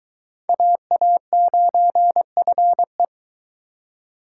Morse code, heard as a keyed tone, about 20 wpm, 700 Hz. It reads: AA9FE